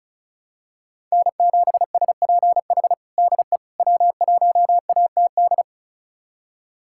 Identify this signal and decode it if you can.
Morse code, heard as a keyed tone, about 36 wpm, 700 Hz. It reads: N7SPH DE W1ATD
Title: USA Callsign DE Callsign - 5 Characters - 35wpm